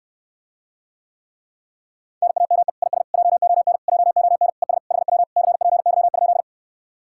Morse code, heard as a keyed tone, about 69 wpm, 700 Hz. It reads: NAME IS JON JON ES HW CPY?